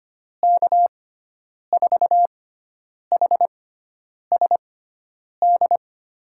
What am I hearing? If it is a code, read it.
Morse code, heard as a keyed tone, about 25 wpm, spaced as for 9 wpm, 700 Hz. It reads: K4HSD
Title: USA Call Signs - 6 Characters - 6x Character Spacing - 25wpm